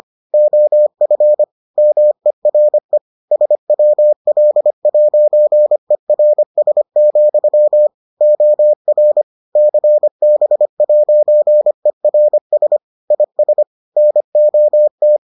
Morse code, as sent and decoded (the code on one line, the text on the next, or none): --- ..-. / -- . .-. / ... .-- .-.. .----. .-. ... --..-- / --- .-. / -.-. -... .----. .-. ... / .. ... / -. --- -
OF MERE SWL'ERS, OR CB'ERS IS NOT